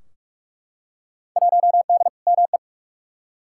Morse code, heard as a keyed tone, about 42 wpm, 700 Hz. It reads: 1D ME
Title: Field Day Exchanges - 45wpm